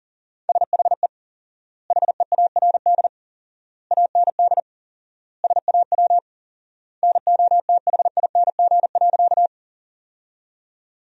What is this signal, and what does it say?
Morse code, heard as a keyed tone, about 40 wpm, 700 Hz. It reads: SHE HEARD AND SAW NOTHING.
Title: 4x Word-Spacing - Sentences from Top 500 Words - 40wpm